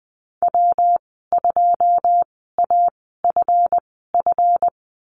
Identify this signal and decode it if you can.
Morse code, heard as a keyed tone, about 20 wpm, spaced as for 15 wpm, 700 Hz. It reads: W2AFF
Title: 167 Lesson 27g - Number 7 - Call Signs 20wpm